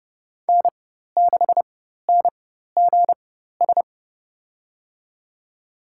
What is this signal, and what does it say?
Morse code, heard as a keyed tone, about 32 wpm, spaced as for 14 wpm, 700 Hz. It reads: N6NGS
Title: USA Call Signs - 5 Characters - 4x Character Spacing - 30wpm